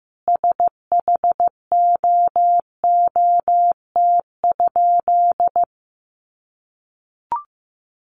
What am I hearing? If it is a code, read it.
Morse code, heard as a keyed tone, about 15 wpm, 700 Hz. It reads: SHOOT?
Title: A Sound Of Thunder 15wpm